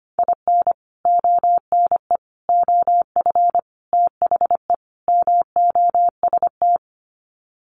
Morse code, heard as a keyed tone, about 25 wpm, 690 Hz. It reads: IN ONE OF THE MOST
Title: Sets of 5 Words Encore 25wpm